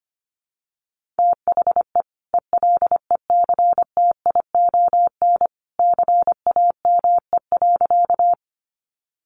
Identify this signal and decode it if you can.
Morse code, heard as a keyed tone, about 25 wpm, 700 Hz. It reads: THE ELECTION CAME.
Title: Sentences from Top 1500 Words 25wpm